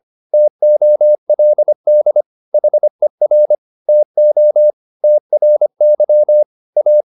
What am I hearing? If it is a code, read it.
Morse code, heard as a keyed tone, about 25 wpm, 600 Hz. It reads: TOLD HER TO TRY A